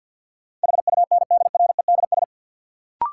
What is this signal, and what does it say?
Morse code, heard as a keyed tone, about 52 wpm, 700 Hz. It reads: HUNDREDS
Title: Sets of 4 Words - New Word Drill - 50wpm